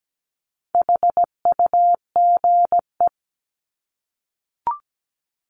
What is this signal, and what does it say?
Morse code, heard as a keyed tone, about 17 wpm, 700 Hz. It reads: HUGE
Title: Top 1001-1250 Words 17wpm